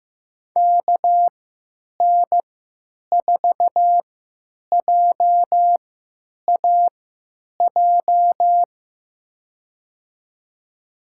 Morse code, heard as a keyed tone, about 15 wpm, spaced as for 9 wpm, 700 Hz. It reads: KN4JAJ
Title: USA Call Signs - 6 Characters - 3x Character Spacing - 15wpm